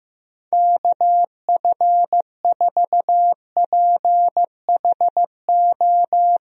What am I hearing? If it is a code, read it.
Morse code, heard as a keyed tone, about 15 wpm, 700 Hz. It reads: KF4PHO
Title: USA Call Signs - 6 Characters - Repeated 3x - 15wpm